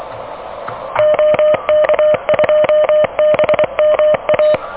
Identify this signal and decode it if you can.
Morse code, heard as a keyed tone, about 18 wpm, 600 Hz. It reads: OK2BMA
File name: OK QRP 2023 / OK QRP 2023